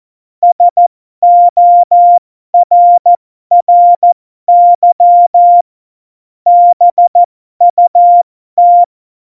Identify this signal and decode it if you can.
Morse code, heard as a keyed tone, about 14 wpm, spaced as for 12 wpm, 700 Hz. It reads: SORRY BUT
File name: Morsekurs / Witze